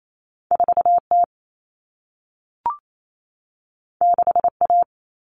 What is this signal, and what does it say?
Morse code, heard as a keyed tone, about 28 wpm, 700 Hz. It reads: 4T 6A
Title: Common Cut Numbers - 2-Digit 28wpm